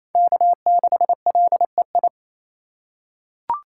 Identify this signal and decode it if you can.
Morse code, heard as a keyed tone, about 28 wpm, 700 Hz. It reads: K6LEI
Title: USA Call Signs - 2x Character Spacing - 28wpm